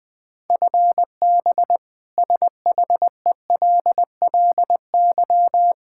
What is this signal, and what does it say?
Morse code, heard as a keyed tone, about 20 wpm, 700 Hz. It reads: FB SHELLY
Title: K1USN SST QSO Practice 20wpm